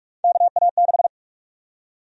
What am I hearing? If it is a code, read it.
Morse code, heard as a keyed tone, about 46 wpm, 700 Hz. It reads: KA6
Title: USA Call Sign Prefixes - 45wpm